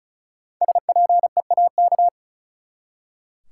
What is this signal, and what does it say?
Morse code, heard as a keyed tone, about 34 wpm, 700 Hz. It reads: SPEAK
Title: Five Letter Words - 35wpm